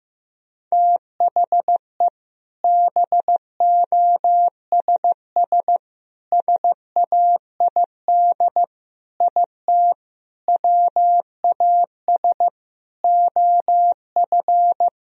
Morse code, heard as a keyed tone, about 15 wpm, 700 Hz. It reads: THE BOSS SAID IT WAS OF
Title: Sentences from Top 1000 Words - Review Part 4 - 15wpm